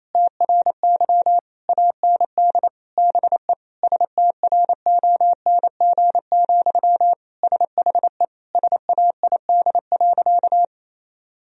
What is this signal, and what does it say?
Morse code, heard as a keyed tone, about 28 wpm, 700 Hz. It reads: TRY AND BE STRONG, SHE SAID.
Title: Sentences from Top 1000 Words - Review Part 3 - 28wpm